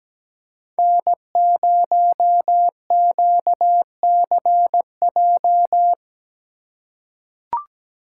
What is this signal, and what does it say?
Morse code, heard as a keyed tone, about 17 wpm, 700 Hz. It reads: N0QCJ
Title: USA Call Signs - 5 Characters - 3x Character Spacing - 17wpm